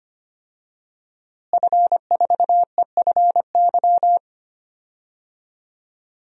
Morse code, heard as a keyed tone, about 25 wpm, 700 Hz. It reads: F4EFY
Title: DXCC-100 Call Signs - 5 Characters - Repeated 3x - 25wpm